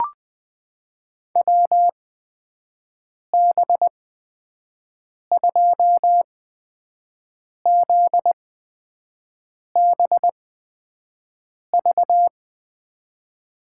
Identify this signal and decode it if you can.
Morse code, heard as a keyed tone, about 21 wpm, spaced as for 5 wpm, 700 Hz. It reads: WB2ZBV
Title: USA Call Signs - 6 Characters - 8x Character Spacing - 20wpm